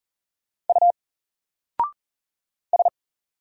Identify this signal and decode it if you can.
Morse code, heard as a keyed tone, about 40 wpm, 700 Hz. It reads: U S
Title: Single Letter-Number 40wpm